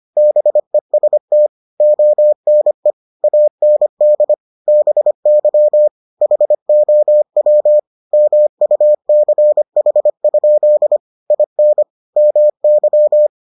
Morse code, heard as a keyed tone, about 25 wpm, 600 Hz. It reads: BEST ONE AND BY HOW MUCH? IN MY